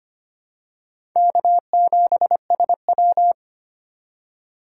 Morse code, heard as a keyed tone, about 25 wpm, 700 Hz. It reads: K7SW
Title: USA Call Signs - 4x Character Spacing - 25wpm